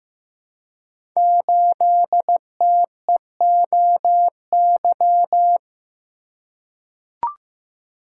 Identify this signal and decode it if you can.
Morse code, heard as a keyed tone, about 15 wpm, 700 Hz. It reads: 8TEOY
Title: Code Groups - 5 Letter:Numbers - 15wpm